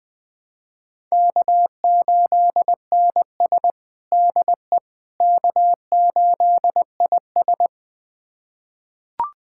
Morse code, heard as a keyed tone, about 20 wpm, 700 Hz. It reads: K8NS DE K8IS
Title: USA Callsign DE Callsign - 4 Characters - 20wpm